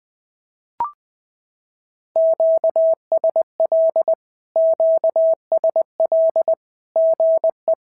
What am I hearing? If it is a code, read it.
Morse code, heard as a keyed tone, about 20 wpm, 650 Hz. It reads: QSL QSL GE
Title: SOTA QSO Practice 20wpm